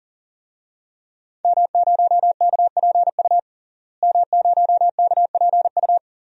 Morse code, heard as a keyed tone, about 40 wpm, 700 Hz. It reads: M0KPU M0KPU
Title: DXCC-100 Call Signs - 5 Characters - Repeated 3x - 40wpm